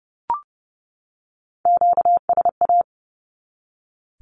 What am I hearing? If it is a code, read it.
Morse code, heard as a keyed tone, about 30 wpm, 700 Hz. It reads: QSA
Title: Q-Codes 30wpm